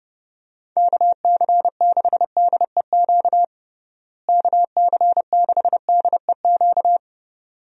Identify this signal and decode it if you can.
Morse code, heard as a keyed tone, about 32 wpm, 700 Hz. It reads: KC6DEQ KC6DEQ
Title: USA Call Signs - 6 Characters - Repeated 2x - 30wpm